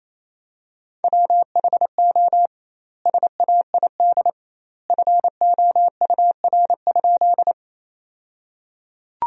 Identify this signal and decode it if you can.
Morse code, heard as a keyed tone, about 28 wpm, 700 Hz. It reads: WHO SAID FOUR?
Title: 2x Word-Spacing - Sentences from Top 300 Words - 28wpm